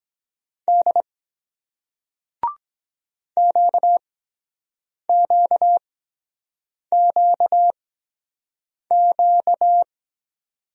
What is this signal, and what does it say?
Morse code, heard as a keyed tone, about 22 wpm, spaced as for 7 wpm, 700 Hz. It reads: D QQQQ